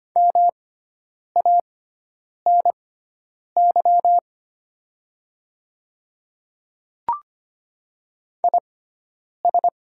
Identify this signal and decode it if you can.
Morse code, heard as a keyed tone, about 25 wpm, spaced as for 9 wpm, 700 Hz. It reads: MANY IS
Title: 6x Character Spacing - Top 100 Words - 25wpm